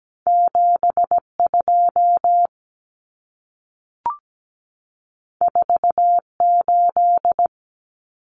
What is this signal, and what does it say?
Morse code, heard as a keyed tone, about 17 wpm, 700 Hz. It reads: 72 48
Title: Numbers - 2-Digit 17wpm